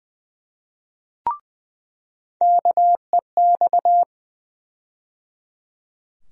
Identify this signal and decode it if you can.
Morse code, heard as a keyed tone, about 21 wpm, 700 Hz. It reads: KEX
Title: USA Call Sign Postfixes - 3 Letters 20wpm